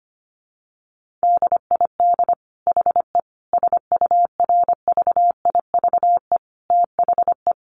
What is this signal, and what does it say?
Morse code, heard as a keyed tone, about 25 wpm, 700 Hz. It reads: DID HE SURVIVE THE
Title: Sentences from Top 1300 Words 25wpm